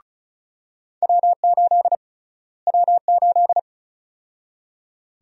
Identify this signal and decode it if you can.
Morse code, heard as a keyed tone, about 36 wpm, 700 Hz. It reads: W8 W8
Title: USA Call Sign Prefixes - Repeated 2x - 35wpm